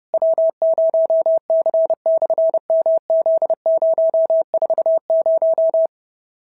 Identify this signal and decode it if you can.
Morse code, heard as a keyed tone, about 30 wpm, 650 Hz. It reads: W0C/MZ040
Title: SOTA QSO Practice 30wpm